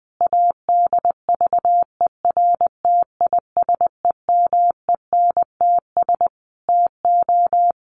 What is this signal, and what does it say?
Morse code, heard as a keyed tone, about 20 wpm, 700 Hz. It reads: ADVERTISEMENTS TO
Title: A Sound Of Thunder 20wpm